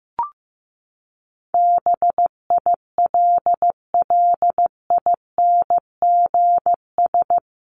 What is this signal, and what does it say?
Morse code, heard as a keyed tone, about 15 wpm, 700 Hz. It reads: BILLINGS
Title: A Sound Of Thunder 15wpm